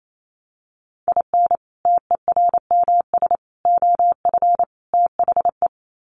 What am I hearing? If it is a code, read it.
Morse code, heard as a keyed tone, about 28 wpm, 700 Hz. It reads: IN TERMS OF THE
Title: Sets of 4 Words 28wpm